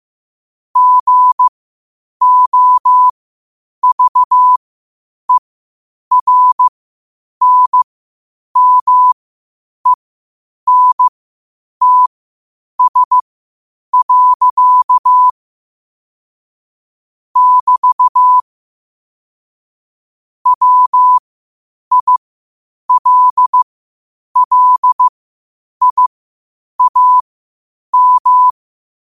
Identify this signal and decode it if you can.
Morse code, heard as a keyed tone, about 15 wpm, spaced as for 8 wpm, 1000 Hz. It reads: GOVERNMENTS. <BT> WILLIAM
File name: qotdfiles / 8wpm